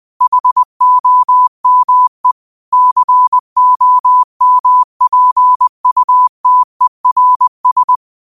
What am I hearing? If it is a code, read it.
Morse code, heard as a keyed tone, about 20 wpm, 1000 Hz. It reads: HOME COMPUTERS
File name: qotdfiles / 20wpm